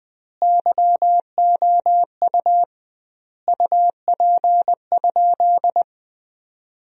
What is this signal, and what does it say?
Morse code, heard as a keyed tone, about 21 wpm, 700 Hz. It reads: YOU UP?
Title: 2x Word-Spacing - Sentences from Top 500 Words - 20wpm